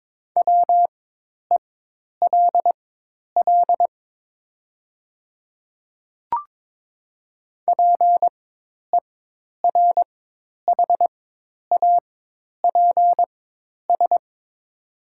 Morse code, heard as a keyed tone, about 22 wpm, spaced as for 10 wpm, 700 Hz. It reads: WELL PERHAPS
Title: 4x Character Spacing - Top 500 Words - 22wpm